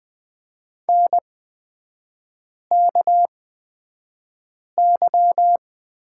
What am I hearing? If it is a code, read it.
Morse code, heard as a keyed tone, about 20 wpm, spaced as for 5 wpm, 690 Hz. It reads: NKY
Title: 229 Lesson 37e - Letter x - Three Characters 20wpm